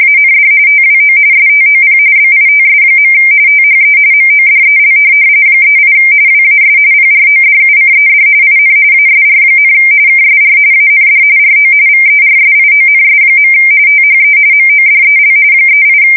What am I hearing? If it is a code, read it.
RTTY, about 45.45 baud. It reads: AND TWO STOP BITS. THE MODEL 38 WAS PRODUCED IN ASR, KSR AND RO VERSIONS. THE MODEL 38
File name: fsk170 / teletype-corp